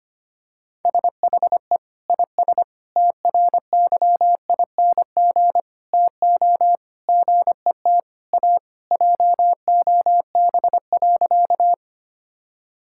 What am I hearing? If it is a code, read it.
Morse code, heard as a keyed tone, about 25 wpm, 700 Hz. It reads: SHE IS TRYING TO GET A JOB.
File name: practice / 25wpm / Sentences with 3-Word Phrases - 25wpm